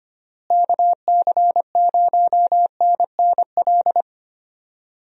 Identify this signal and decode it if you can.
Morse code, heard as a keyed tone, about 25 wpm, 700 Hz. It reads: KC0NNL
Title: USA Call Signs - 6 Characters - Repeated 3x - 25wpm